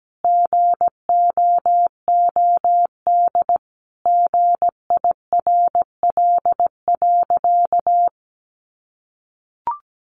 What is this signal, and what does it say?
Morse code, heard as a keyed tone, about 17 wpm, 700 Hz. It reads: GOOD GIRL.
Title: Sentences from Top 400 Words 17wpm